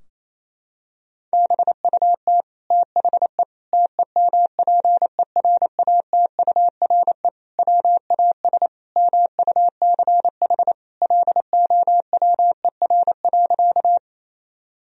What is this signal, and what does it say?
Morse code, heard as a keyed tone, about 28 wpm, 700 Hz. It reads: BUT THE TEMPERATURE WAS MUCH LOWER.
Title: Sentences from Top 1000 Words - Review Part 4 - 28wpm